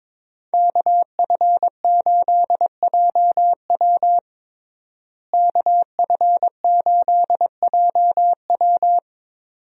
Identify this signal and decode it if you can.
Morse code, heard as a keyed tone, about 22 wpm, 700 Hz. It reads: KF8JW KF8JW
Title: USA Call Signs - 5 Characters - Repeated 2x - 22wpm